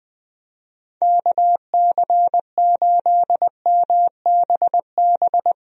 Morse code, heard as a keyed tone, about 20 wpm, 700 Hz. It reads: KC8MBB
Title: USA Call Signs - 6 Characters - Repeated 2x - 20wpm